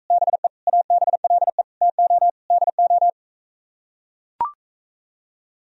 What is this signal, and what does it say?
Morse code, heard as a keyed tone, about 45 wpm, 700 Hz. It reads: BE ABLE TO DO